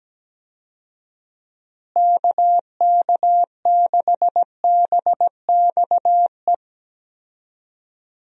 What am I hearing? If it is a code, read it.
Morse code, heard as a keyed tone, about 17 wpm, 700 Hz. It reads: KK6BXE
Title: USA Call Signs - 6 Characters - Repeated 3x - 17wpm